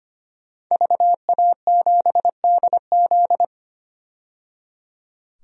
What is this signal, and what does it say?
Morse code, heard as a keyed tone, about 25 wpm, 700 Hz. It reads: VA7DZ
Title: DXCC-100 Call Signs - 5 Characters - 25wpm